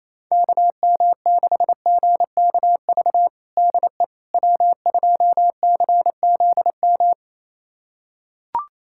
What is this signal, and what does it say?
Morse code, heard as a keyed tone, about 27 wpm, 700 Hz. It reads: KM6GKV DE W2CZM
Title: USA Callsign DE Callsign 28wpm